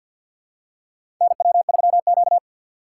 Morse code, heard as a keyed tone, about 51 wpm, 700 Hz. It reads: NW3Q
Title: USA Call Signs - 4 Characters - 3x Character Spacing - 50wpm